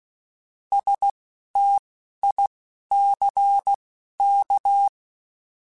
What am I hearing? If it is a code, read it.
Morse code, heard as a keyed tone, about 16 wpm, spaced as for 11 wpm, 800 Hz. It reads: STICK
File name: Alphabetical / Alphabetical mp3s / Ss @10